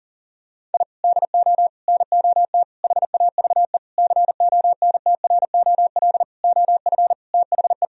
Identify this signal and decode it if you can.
Morse code, heard as a keyed tone, about 40 wpm, 700 Hz. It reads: I DO NOT HAVE CONTROL OF THE
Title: Sentences from Top 1000 Words - Review Part 3 - 40wpm